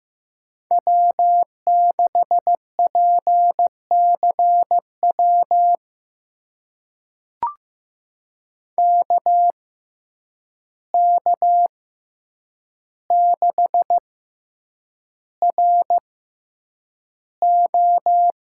Morse code, heard as a keyed tone, about 15 wpm, spaced as for 8 wpm, 700 Hz. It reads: W6PCW KK6RO
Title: USA Call Signs - 5 Characters - 6x Character Spacing - 15wpm